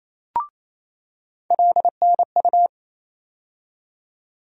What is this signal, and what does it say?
Morse code, heard as a keyed tone, about 28 wpm, 700 Hz. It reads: LNU
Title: USA Call Sign Postfixes - 3 Letters 28wpm